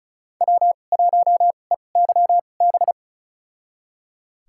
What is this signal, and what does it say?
Morse code, heard as a keyed tone, about 36 wpm, spaced as for 25 wpm, 700 Hz. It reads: W1EYB
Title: USA Call Signs - 5 Characters - 2x Character Spacing - 35wpm